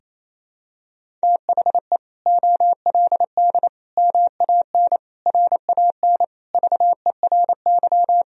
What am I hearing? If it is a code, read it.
Morse code, heard as a keyed tone, about 28 wpm, 700 Hz. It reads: THE OLD MAN RAN VERY
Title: Sentences from Top 1000 Words - Review Part 3 - 28wpm